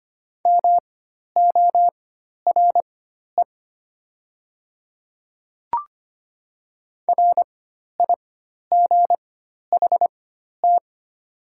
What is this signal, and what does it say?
Morse code, heard as a keyed tone, about 25 wpm, spaced as for 12 wpm, 700 Hz. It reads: MORE RIGHT
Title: 4x Character Spacing - Top 300 Words - 25wpm